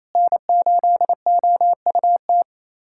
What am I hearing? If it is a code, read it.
Morse code, heard as a keyed tone, about 27 wpm, 700 Hz. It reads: N8OUT
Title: USA Callsign DE Callsign - 5 Characters - 28wpm